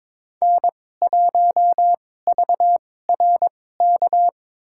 Morse code, heard as a keyed tone, about 22 wpm, spaced as for 16 wpm, 700 Hz. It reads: N1VRK
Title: USA Call Signs - 5 Characters - 2x Character Spacing - 22wpm